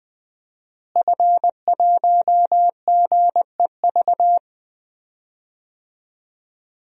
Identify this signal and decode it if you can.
Morse code, heard as a keyed tone, about 20 wpm, 700 Hz. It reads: F1GEV
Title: DXCC-100 Call Signs - 5 Characters - Repeated 3x - 20wpm